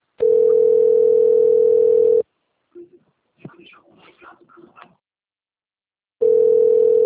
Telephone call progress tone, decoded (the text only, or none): ringback tone